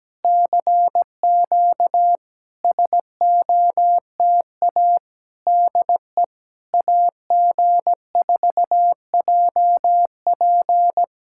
Morse code, heard as a keyed tone, about 17 wpm, 700 Hz. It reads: CQ SOTA DE AG4JP
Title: SOTA QSO Practice 17wpm